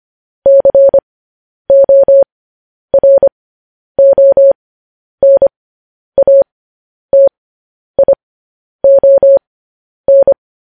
Morse code, heard as a keyed tone, about 25 wpm, spaced as for 10 wpm, 550 Hz. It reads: CORONATION